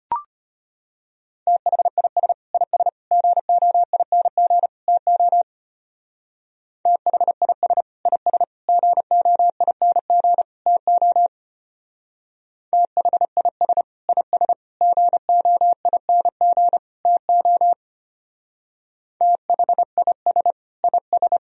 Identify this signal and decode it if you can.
Morse code, heard as a keyed tone, about 32 wpm, 700 Hz. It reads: THIS IS GOING TO THIS IS GOING TO THIS IS GOING TO THIS IS